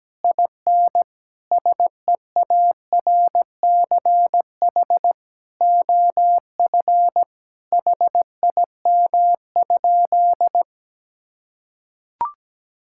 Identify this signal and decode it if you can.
Morse code, heard as a keyed tone, about 17 wpm, 700 Hz. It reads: IN SEARCH OF HIM?
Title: Sentences from Top 1000 Words - Review Part 5 - 17wpm